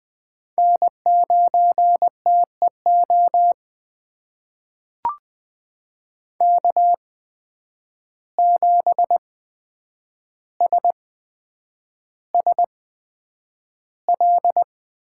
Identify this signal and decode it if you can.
Morse code, heard as a keyed tone, about 20 wpm, spaced as for 9 wpm, 700 Hz. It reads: N9TEO K7SSL
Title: USA Call Signs - 5 Characters - 8x Character Spacing - 20wpm